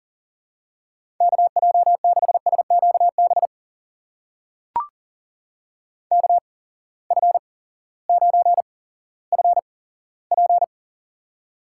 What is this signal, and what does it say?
Morse code, heard as a keyed tone, about 39 wpm, spaced as for 18 wpm, 700 Hz. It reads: KJ6SQB KF9FP
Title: USA Call Signs - 8x Character Spacing - 40wpm